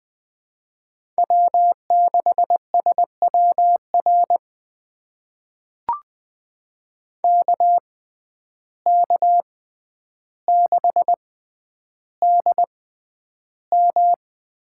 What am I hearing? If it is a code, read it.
Morse code, heard as a keyed tone, about 20 wpm, spaced as for 10 wpm, 700 Hz. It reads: W6SWR KK6DM
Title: USA Call Signs - 5 Characters - 6x Character Spacing - 20wpm